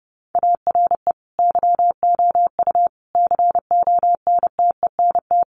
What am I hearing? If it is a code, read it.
Morse code, heard as a keyed tone, about 30 wpm, 700 Hz. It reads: ARE YOU CONTENT
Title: Sentences from Top 900 Words 30wpm